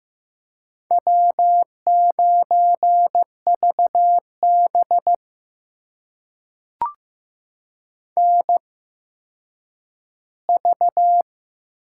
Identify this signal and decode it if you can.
Morse code, heard as a keyed tone, about 15 wpm, spaced as for 9 wpm, 700 Hz. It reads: W9VB NV
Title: USA Call Signs - 4 Characters - 8x Character Spacing - 15wpm